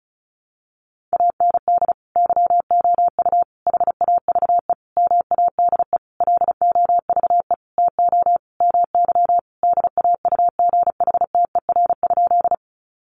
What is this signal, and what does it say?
Morse code, heard as a keyed tone, about 34 wpm, 700 Hz. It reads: AND YOU HAVE MADE LOVE TO MY DAUGHTER?
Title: Sentences from Top 1500 Words 35wpm